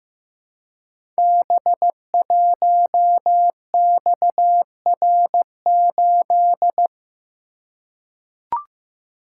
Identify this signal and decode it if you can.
Morse code, heard as a keyed tone, about 15 wpm, 700 Hz. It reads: B1XR8
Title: Code Groups - 5 Letter:Numbers - 15wpm